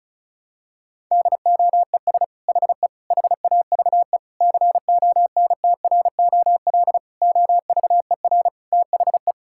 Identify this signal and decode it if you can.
Morse code, heard as a keyed tone, about 36 wpm, 700 Hz. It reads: DOES HE HAVE CONTROL OVER THE
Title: Sentences from Top 1000 Words - Review Part 5 - 35wpm